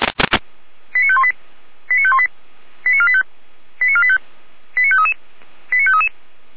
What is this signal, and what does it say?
Five-tone selective call (ZVEI), heard as quick runs of five tones, tones of about 70 ms. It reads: 89518 89518 89485 89485 89533 89533